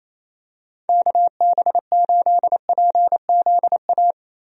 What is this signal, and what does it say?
Morse code, heard as a keyed tone, about 28 wpm, 700 Hz. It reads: KB8PZA
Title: USA Call Signs - 6 Characters - 28wpm